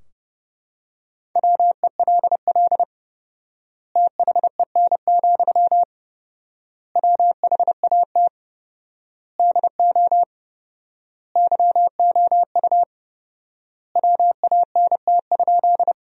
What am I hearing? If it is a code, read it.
Morse code, heard as a keyed tone, about 30 wpm, 700 Hz. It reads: WELL THEN, WHAT DO YOU WANT?
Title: 4x Word-Spacing - Sentences from Top 200 Words - 30wpm